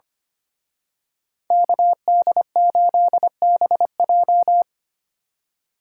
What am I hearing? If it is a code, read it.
Morse code, heard as a keyed tone, about 30 wpm, 700 Hz. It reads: KD8BJ
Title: Speed Racing - USA 2x2 Call Signs 25wpm